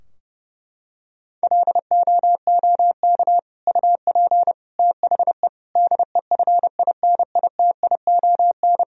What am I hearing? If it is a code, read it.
Morse code, heard as a keyed tone, about 32 wpm, 700 Hz. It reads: LOOK UP THE DEFINITION
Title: Sentences from Top 1000 Words - Review Part 4 - 30wpm